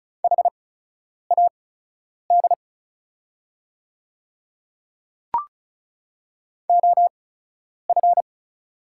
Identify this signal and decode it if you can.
Morse code, heard as a keyed tone, about 36 wpm, spaced as for 10 wpm, 700 Hz. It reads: HAD OF
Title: 8x Character Spacing - Top 100 Words - 35wpm